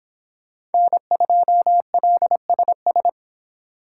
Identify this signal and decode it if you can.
Morse code, heard as a keyed tone, about 31 wpm, 700 Hz. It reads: N2LSS